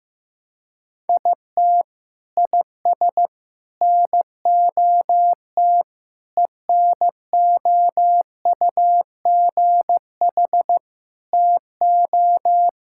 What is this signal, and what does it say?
Morse code, heard as a keyed tone, about 16 wpm, 700 Hz. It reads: IT IS NOT ENOUGH TO
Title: Sentences from Top 1000 Words - Review Part 6 - 15wpm